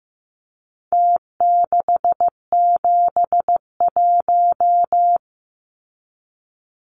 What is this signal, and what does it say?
Morse code, heard as a keyed tone, about 15 wpm, 700 Hz. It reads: T671
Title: Numbers - Contest Style - 4-Digit 15wpm